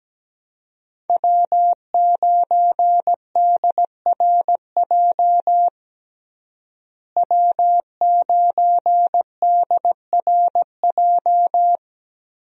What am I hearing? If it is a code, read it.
Morse code, heard as a keyed tone, about 17 wpm, 700 Hz. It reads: W9DRJ W9DRJ
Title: USA Call Signs - 5 Characters - Repeated 3x - 17wpm